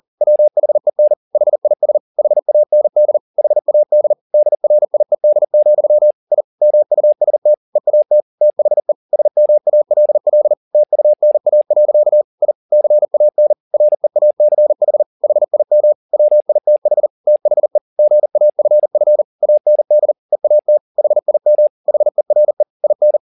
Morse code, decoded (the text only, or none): WHEN HIS HAND HAD DRIED, I MUST EAT THE SMALL TUNA. I CAN REACH HIM WITH THE GAFF AND EAT HIM HERE IN